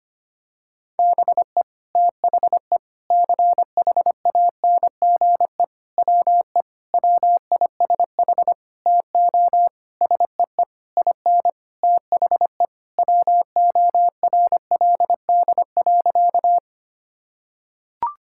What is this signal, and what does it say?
Morse code, heard as a keyed tone, about 25 wpm, 700 Hz. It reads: BE THE CHANGE WE WISH TO SEE IN THE WORLD.
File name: practice / 25wpm / Sentences with 9-Word Phrases - 25wpm